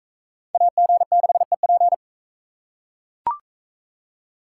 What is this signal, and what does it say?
Morse code, heard as a keyed tone, about 42 wpm, 700 Hz. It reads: AG6EP